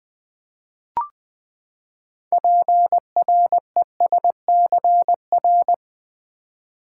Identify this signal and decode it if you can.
Morse code, heard as a keyed tone, about 20 wpm, 700 Hz. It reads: PRESCR
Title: Build A Word - 6 Letters At A Time - 12-Letter Words - 20wpm